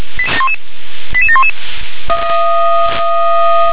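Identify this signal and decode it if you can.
Five-tone selective call (ZVEI), heard as quick runs of five tones, tones of about 70 ms. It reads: 88711
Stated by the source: ZVEI five-tone selective call 88704 received on SDR.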